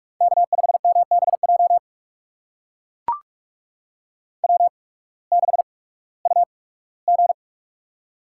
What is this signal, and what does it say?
Morse code, heard as a keyed tone, about 44 wpm, spaced as for 21 wpm, 700 Hz. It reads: K5MBJ W6UG
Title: USA Call Signs - 8x Character Spacing - 45wpm